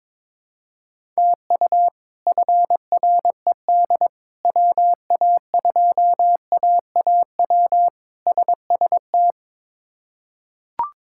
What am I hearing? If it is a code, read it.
Morse code, heard as a keyed tone, about 22 wpm, 700 Hz. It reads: TU FRED WA2AAW SST
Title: K1USN SST QSO Practice - 22wpm